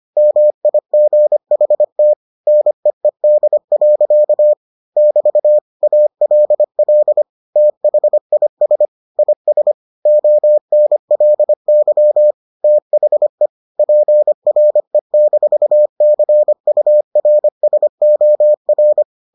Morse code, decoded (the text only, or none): MIGHT NEED. <BT> ALL THIS IS ONLY THE PRE-CURSOR